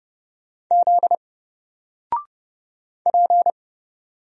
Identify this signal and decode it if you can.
Morse code, heard as a keyed tone, about 30 wpm, 700 Hz. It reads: Z P
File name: Single-Letter-Number-30wpm